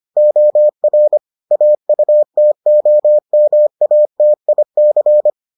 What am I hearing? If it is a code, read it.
Morse code, heard as a keyed tone, about 25 wpm, 600 Hz. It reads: OR AUTOMATIC